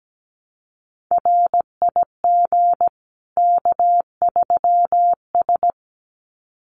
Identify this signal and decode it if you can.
Morse code, heard as a keyed tone, about 17 wpm, 700 Hz. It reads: RIG K3S
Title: Common QSO Phrases 17wpm